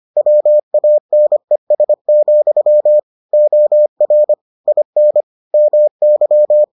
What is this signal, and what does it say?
Morse code, heard as a keyed tone, about 25 wpm, 600 Hz. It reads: WANES, OR IN MY